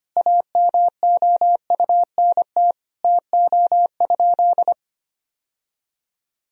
Morse code, heard as a keyed tone, about 25 wpm, 700 Hz. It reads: AMOUNT TO?
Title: Sentences from Top 1000 Words - Review Part 4 - 25wpm